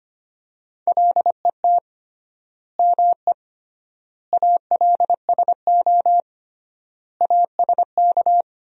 Morse code, heard as a keyed tone, about 25 wpm, 700 Hz. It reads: LET ME ALSO ASK
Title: 3x Word-Spacing - Sentences from Top 400 Words - 25wpm